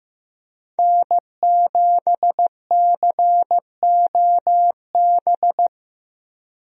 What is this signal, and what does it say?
Morse code, heard as a keyed tone, about 15 wpm, 700 Hz. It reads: N7COB
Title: USA Call Signs - 5 Characters - Repeated 3x - 15wpm